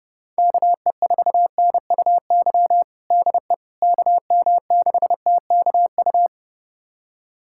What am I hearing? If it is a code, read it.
Morse code, heard as a keyed tone, about 31 wpm, 700 Hz. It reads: KE4NUY DE KM6TKU
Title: USA Callsign DE Callsign 30wpm